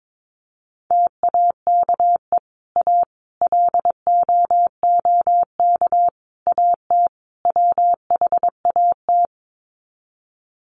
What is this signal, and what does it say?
Morse code, heard as a keyed tone, about 22 wpm, 700 Hz. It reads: TAKE A LOOK AT WHAT
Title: Sets of 5 Words Encore2 22wpm